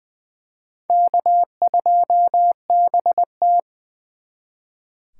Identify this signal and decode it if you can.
Morse code, heard as a keyed tone, about 24 wpm, 700 Hz. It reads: K2BT
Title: Speed Racing - USA 1x2 Call Signs 20wpm